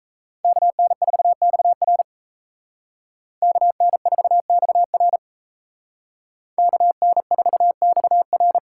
Morse code, heard as a keyed tone, about 33 wpm, 700 Hz. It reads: KN4XR KN4XR KN4XR
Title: Speed Racing - USA 2x2 Call Signs 28wpm